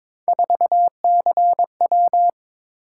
Morse code, heard as a keyed tone, about 22 wpm, 700 Hz. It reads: 4CW
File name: practice / 22wpm / Three Letter-Number Combinations - 22wpm